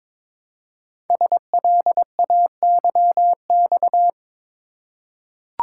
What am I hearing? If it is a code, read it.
Morse code, heard as a keyed tone, about 22 wpm, 700 Hz. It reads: SLAYX
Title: Code Groups - 5 Letters - 22wpm